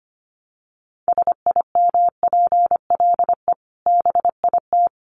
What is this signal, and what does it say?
Morse code, heard as a keyed tone, about 25 wpm, 700 Hz. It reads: SIMPLE BIT
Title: Sentences from Top 900 Words 25wpm